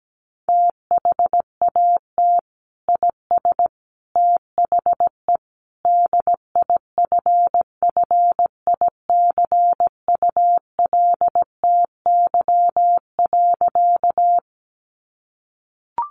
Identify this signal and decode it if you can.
Morse code, heard as a keyed tone, about 17 wpm, 700 Hz. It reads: THAT IS THE DIFFICULTY.
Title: Sentences from Top 1500 Words 17wpm